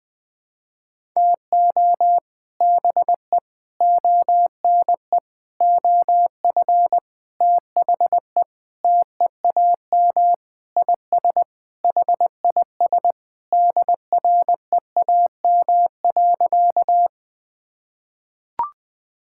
Morse code, heard as a keyed tone, about 20 wpm, 700 Hz. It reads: TO BE ONE OF THE TEAM IS HIS DREAM.
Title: Sentences with 4-Word Phrases - Spoken Beginning - 20wpm